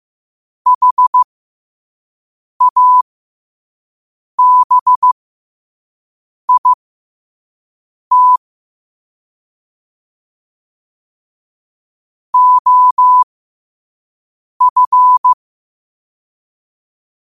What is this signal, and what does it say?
Morse code, heard as a keyed tone, about 15 wpm, spaced as for 5 wpm, 1000 Hz. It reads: HABIT OF
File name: qotdfiles / 5wpm